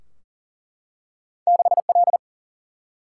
Morse code, heard as a keyed tone, about 40 wpm, 700 Hz. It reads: 6L
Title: Single-Letter Single-Number 40wpm